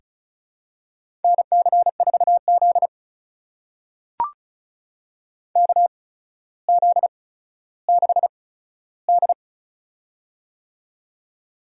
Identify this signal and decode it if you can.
Morse code, heard as a keyed tone, about 35 wpm, spaced as for 15 wpm, 700 Hz. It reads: NC4Z KZ6D
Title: USA Call Signs - 4 Characters - 8x Character Spacing - 35wpm